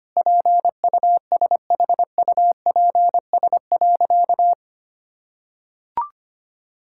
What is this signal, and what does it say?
Morse code, heard as a keyed tone, about 25 wpm, 700 Hz. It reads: PUSHUPS.
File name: Easy Sentences with Hints - 25wpm